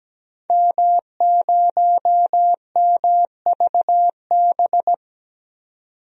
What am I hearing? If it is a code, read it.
Morse code, heard as a keyed tone, about 17 wpm, 700 Hz. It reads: M0MVB
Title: DXCC-100 Call Signs - 5 Characters - 17wpm